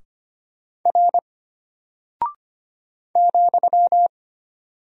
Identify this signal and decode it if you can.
Morse code, heard as a keyed tone, about 25 wpm, 700 Hz. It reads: R ,
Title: Single Character (Letter:Number:Punctuation:Prosign) Alt 25wpm